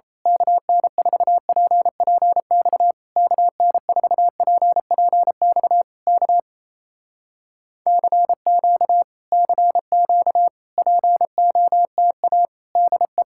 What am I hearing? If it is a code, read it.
Morse code, heard as a keyed tone, about 27 wpm, 700 Hz. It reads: KN4PPX KN4PPX K CQ CQ POTA DE
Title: Speed-Racing Common QSO Phrases - 28wpm